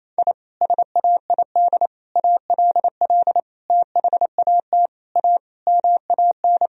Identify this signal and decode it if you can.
Morse code, heard as a keyed tone, about 28 wpm, 700 Hz. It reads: I SAID ALL THAT A MAN
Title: Sentences from Top 1000 Words - Review Part 2 - 28wpm